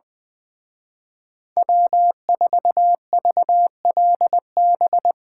Morse code, heard as a keyed tone, about 22 wpm, 700 Hz. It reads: W4VLB
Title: Speed Racing - USA 1x3 Call Signs 20wpm